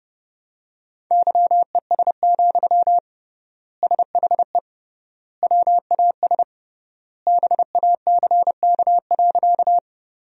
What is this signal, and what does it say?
Morse code, heard as a keyed tone, about 30 wpm, 700 Hz. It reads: YES, SHE WAS BACK.
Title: 3x Word-Spacing - Sentences from Top 500 Words - 30wpm